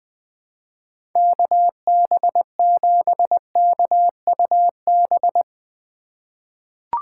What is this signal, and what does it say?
Morse code, heard as a keyed tone, about 20 wpm, 700 Hz. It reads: KB7KUB
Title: USA Call Signs - 6 Characters - 2x Character Spacing - 20wpm